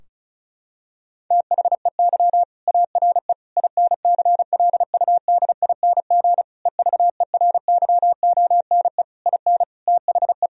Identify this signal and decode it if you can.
Morse code, heard as a keyed tone, about 36 wpm, 700 Hz. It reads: THEY ARE INCLUDING EVERYONE IN THE
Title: Sentences from Top 1000 Words - Review Part 4 - 35wpm